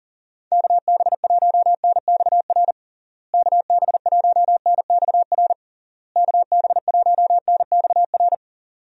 Morse code, heard as a keyed tone, about 37 wpm, 700 Hz. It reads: KB1NXR KB1NXR KB1NXR
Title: USA Call Signs - 6 Characters - Repeated 3x - 40wpm